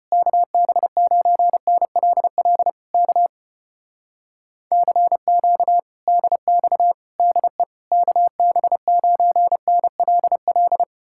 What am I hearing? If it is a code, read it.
Morse code, heard as a keyed tone, about 33 wpm, 700 Hz. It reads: KB9NLL K CQ DX DE KB9NLL